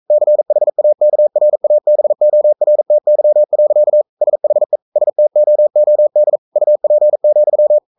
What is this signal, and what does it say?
Morse code, heard as a keyed tone, about 42 wpm, 600 Hz. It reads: CHAKRABORTY. SHE STOOD UP,
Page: Romanen Never af Ken Follet på engelsk.